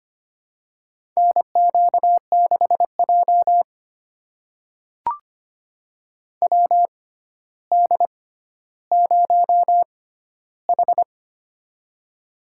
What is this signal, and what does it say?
Morse code, heard as a keyed tone, about 25 wpm, spaced as for 13 wpm, 700 Hz. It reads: NQ6J WD0H
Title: USA Call Signs - 4 Characters - 6x Character Spacing - 25wpm